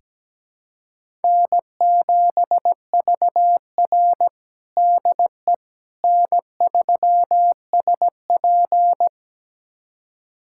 Morse code, heard as a keyed tone, about 17 wpm, 700 Hz. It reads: N7VR DE N3SP
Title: USA Callsign DE Callsign - 4 Characters - 17wpm